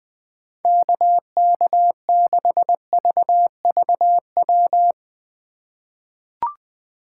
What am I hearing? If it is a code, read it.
Morse code, heard as a keyed tone, about 21 wpm, 700 Hz. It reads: KK6VVW
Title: USA Call Signs - 6 Characters - 2x Character Spacing - 20wpm